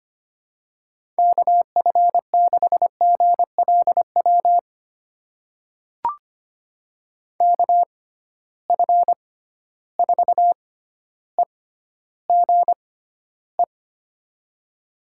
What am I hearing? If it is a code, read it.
Morse code, heard as a keyed tone, about 25 wpm, spaced as for 13 wpm, 700 Hz. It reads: KF6GLW KF4EGE
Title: USA Call Signs - 6 Characters - 6x Character Spacing - 25wpm